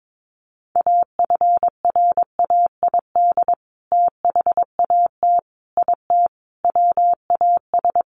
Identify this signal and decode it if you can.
Morse code, heard as a keyed tone, about 22 wpm, 700 Hz. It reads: AFRAID THAT IT WAS
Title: Sentences from Top 1500 Words 22wpm